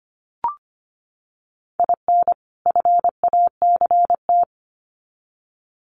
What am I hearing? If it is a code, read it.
Morse code, heard as a keyed tone, about 25 wpm, 700 Hz. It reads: IN FACT
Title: Sets of 2 Words 25wpm